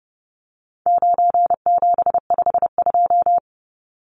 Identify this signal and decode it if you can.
Morse code, heard as a keyed tone, about 30 wpm, 700 Hz. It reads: 9752
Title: Numbers - 4-Digit 30wpm